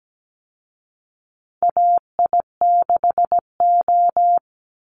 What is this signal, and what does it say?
Morse code, heard as a keyed tone, about 17 wpm, 700 Hz. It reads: AI6O
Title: Call Signs Repeated 3x - 17wpm